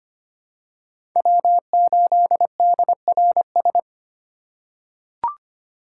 Morse code, heard as a keyed tone, about 25 wpm, 700 Hz. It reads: W8DRS
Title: USA Call Signs - 4x Character Spacing - 25wpm